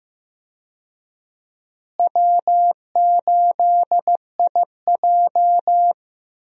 Morse code, heard as a keyed tone, about 15 wpm, 700 Hz. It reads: W8IJ
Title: USA Call Signs - 4 Characters - Repeated 2x - 15wpm